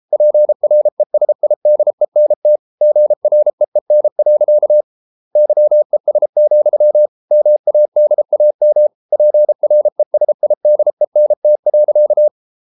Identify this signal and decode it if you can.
Morse code, heard as a keyed tone, about 33 wpm, 600 Hz. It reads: PRESIDENT GREEN. YES, MADAM PRESIDENT.